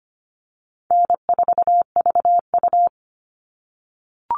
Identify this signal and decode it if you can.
Morse code, heard as a keyed tone, about 25 wpm, 700 Hz. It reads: N4VU
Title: All Cut Numbers - 4-Digit 25wpm